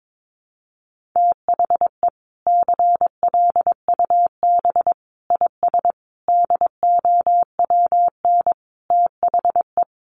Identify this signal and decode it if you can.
Morse code, heard as a keyed tone, about 22 wpm, 700 Hz. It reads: THE CLUB IS DOWN THE
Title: Sentences from Top 1200 Words 22wpm